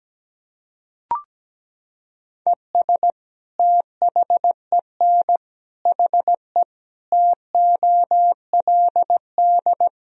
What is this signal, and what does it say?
Morse code, heard as a keyed tone, about 17 wpm, 700 Hz. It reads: ES THEN HE TOLD
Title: Sentences from Top 400 Words and Abbreviations 17wpm